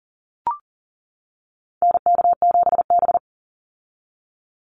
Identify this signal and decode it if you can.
Morse code, heard as a keyed tone, about 40 wpm, 700 Hz. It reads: NK7B
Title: Call Signs - 40wpm